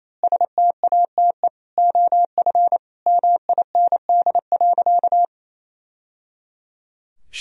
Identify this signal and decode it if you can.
Morse code, heard as a keyed tone, about 28 wpm, 700 Hz. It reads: STATE OF MIND.
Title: Sentences from Top 1000 Words - Review Part 2 - 28wpm